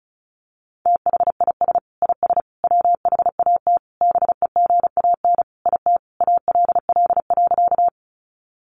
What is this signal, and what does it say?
Morse code, heard as a keyed tone, about 34 wpm, 700 Hz. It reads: THIS IS WHAT BEGAN IT ALL.
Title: Sentences from Top 300 Words 35wpm